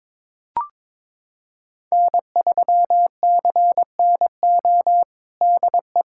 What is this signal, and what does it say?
Morse code, heard as a keyed tone, about 22 wpm, 700 Hz. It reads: N3CNO DE
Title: USA Callsign DE Callsign 22wpm